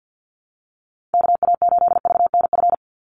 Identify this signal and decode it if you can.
Morse code, heard as a keyed tone, about 50 wpm, 700 Hz. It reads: KA8VNF
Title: USA Call Signs 50wpm